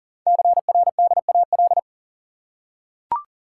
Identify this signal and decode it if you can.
Morse code, heard as a keyed tone, about 40 wpm, 690 Hz. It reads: CRDAL
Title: Code Groups - 5 Letters - 40wpm